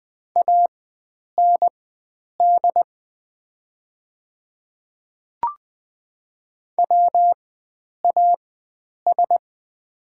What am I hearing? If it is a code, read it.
Morse code, heard as a keyed tone, about 20 wpm, spaced as for 9 wpm, 700 Hz. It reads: AND WAS
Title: 4x Character Spacing - Top 100 Words - 20wpm